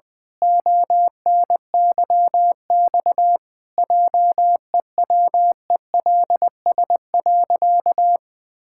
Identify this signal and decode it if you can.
Morse code, heard as a keyed tone, about 20 wpm, 700 Hz. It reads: ONYX JEWELS.
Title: Pangram Challenge v2 - 20wpm